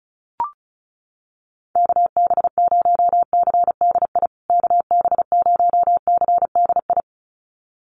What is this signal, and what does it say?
Morse code, heard as a keyed tone, about 34 wpm, 700 Hz. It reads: KB0CDI KB0CDI
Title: USA Call Signs Repeated 2x 35wpm